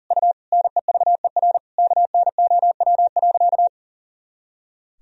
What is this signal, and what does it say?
Morse code, heard as a keyed tone, about 37 wpm, 700 Hz. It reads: U NEVER KNOW.
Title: Sentences from Top 300 Words and Abbreviations 40wpm